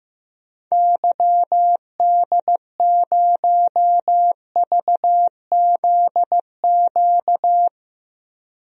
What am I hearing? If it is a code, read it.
Morse code, heard as a keyed tone, about 15 wpm, 700 Hz. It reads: YD0VZQ
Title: DXCC-100 Call Signs - 6 Characters - 15wpm